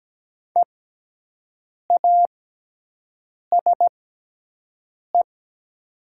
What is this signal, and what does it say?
Morse code, heard as a keyed tone, about 17 wpm, spaced as for 6 wpm, 700 Hz. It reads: EASE
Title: 6x Character Spacing - Top 500 Words - 17wpm